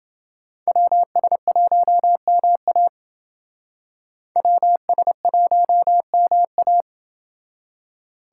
Morse code, heard as a keyed tone, about 28 wpm, 700 Hz. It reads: WS1MA WS1MA